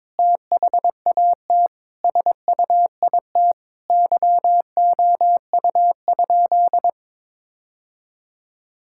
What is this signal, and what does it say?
Morse code, heard as a keyed tone, about 22 wpm, 700 Hz. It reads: THAT SUIT YOU?
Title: Sentences from Top 1600 Words 22wpm